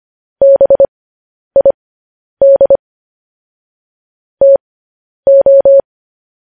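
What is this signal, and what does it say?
Morse code, heard as a keyed tone, about 25 wpm, spaced as for 10 wpm, 550 Hz. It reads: BID TO
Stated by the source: Morse code news headlines sent at 10.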